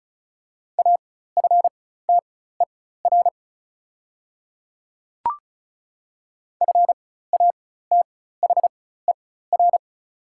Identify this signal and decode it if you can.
Morse code, heard as a keyed tone, about 36 wpm, spaced as for 16 wpm, 700 Hz. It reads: AFTER FATHER
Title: 4x Character Spacing - Top 200 Words - 35wpm